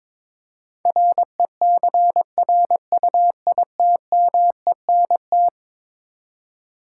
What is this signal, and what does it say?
Morse code, heard as a keyed tone, about 22 wpm, 700 Hz. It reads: RECRUITMENT
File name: practice / 22wpm / Eleven Letter Words - 22wpm